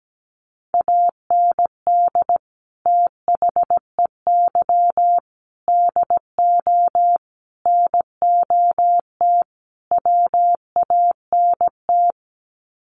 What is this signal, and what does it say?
Morse code, heard as a keyed tone, about 17 wpm, 700 Hz. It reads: AND THEY DO NOT WANT
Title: Sets of 5 Words Encore2 17wpm